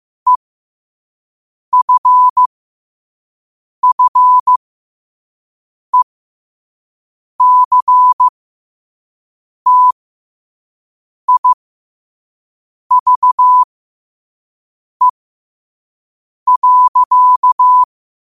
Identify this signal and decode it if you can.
Morse code, heard as a keyed tone, about 15 wpm, spaced as for 5 wpm, 1000 Hz. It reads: EFFECTIVE.
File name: qotdfiles / 5wpm